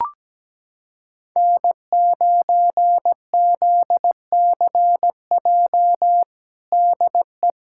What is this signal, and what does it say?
Morse code, heard as a keyed tone, about 17 wpm, 700 Hz. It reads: N9ZCJ DE
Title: USA Callsign DE Callsign - 5 Characters - 17wpm